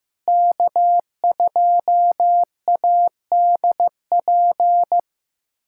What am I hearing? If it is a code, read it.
Morse code, heard as a keyed tone, about 15 wpm, 700 Hz. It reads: K2ADP
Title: USA Callsign DE Callsign - 5 Characters - 15wpm